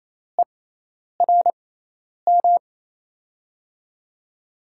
Morse code, heard as a keyed tone, about 28 wpm, spaced as for 10 wpm, 700 Hz. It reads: ERM
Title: USA Call Signs - 5 Characters - 6x Character Spacing - 28wpm